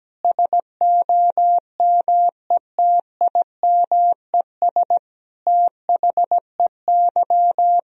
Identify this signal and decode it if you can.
Morse code, heard as a keyed tone, about 17 wpm, 700 Hz. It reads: SOMETIMES THEY
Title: Sentences from Top 1000 Words - Review Part 3 - 17wpm